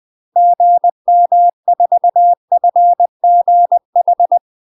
Morse code, heard as a keyed tone, about 20 wpm, 700 Hz. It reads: GM4FGH
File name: UK MASTER QSOs TRAINING / 20WPM / UK MASTER QSOs 0066